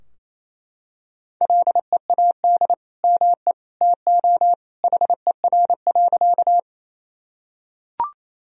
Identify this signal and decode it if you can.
Morse code, heard as a keyed tone, about 28 wpm, 700 Hz. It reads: LEAD ME TO HER.
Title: Sentences from Top 1000 Words - Review Part 2 - 28wpm